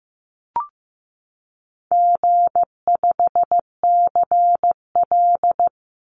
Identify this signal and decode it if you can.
Morse code, heard as a keyed tone, about 15 wpm, 700 Hz. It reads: G5CL
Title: Easy Call Signs Repeated 2x - 15wpm